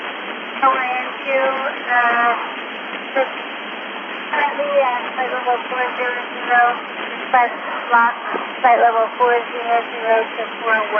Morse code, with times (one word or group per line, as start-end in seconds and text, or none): none